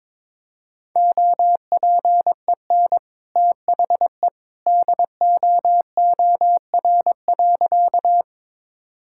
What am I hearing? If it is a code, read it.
Morse code, heard as a keyed tone, about 21 wpm, 700 Hz. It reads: OPEN THE DOOR.
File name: practice / 22wpm / Sentences with 2-Word Phrases - 22wpm